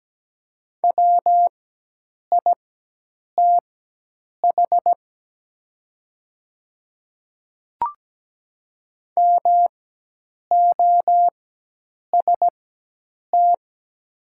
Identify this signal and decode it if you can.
Morse code, heard as a keyed tone, about 17 wpm, spaced as for 8 wpm, 700 Hz. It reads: WITH MOST
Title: 4x Character Spacing - Top 100 Words - 17wpm